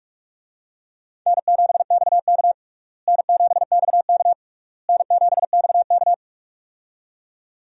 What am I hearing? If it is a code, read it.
Morse code, heard as a keyed tone, about 46 wpm, 700 Hz. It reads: N7XK N7XK N7XK
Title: USA Call Signs - 4 Characters - Repeated 3x - 45wpm